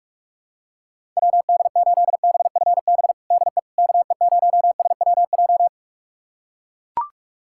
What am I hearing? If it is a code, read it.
Morse code, heard as a keyed tone, about 42 wpm, 700 Hz. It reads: WD8BFB DE KE0SWJ
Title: USA Callsign DE Callsign - 6 Characters - 45wpm